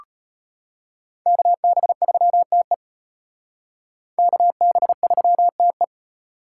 Morse code, heard as a keyed tone, about 32 wpm, 700 Hz. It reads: KB3TE KB3TE